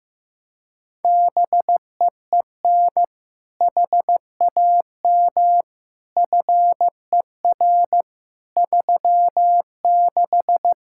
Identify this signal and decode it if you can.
Morse code, heard as a keyed tone, about 15 wpm, 700 Hz. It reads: BEEN HAM FER 36
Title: Practice QSO - Exchange 3 - More Info - 15wpm